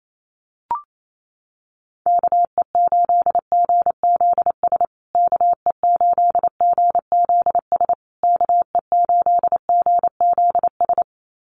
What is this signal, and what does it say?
Morse code, heard as a keyed tone, about 28 wpm, 700 Hz. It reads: KE8GZS KE8GZS KE8GZS
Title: USA Call Signs Repeated 3x 28wpm